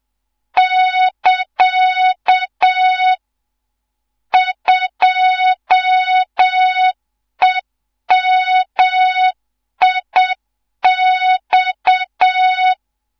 Morse code, heard as a keyed tone, about 7 wpm, 3700 Hz. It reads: <KA> 2EMIX
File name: Morze-HA2MN / hang / 14 280 HA2MN